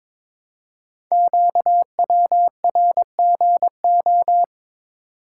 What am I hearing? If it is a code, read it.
Morse code, heard as a keyed tone, about 23 wpm, 700 Hz. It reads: QWRGO
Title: Code Groups - 5 Letters - 22wpm